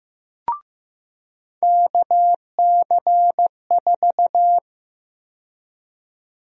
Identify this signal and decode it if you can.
Morse code, heard as a keyed tone, about 15 wpm, 700 Hz. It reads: KC4
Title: USA Call Sign Prefixes - 15wpm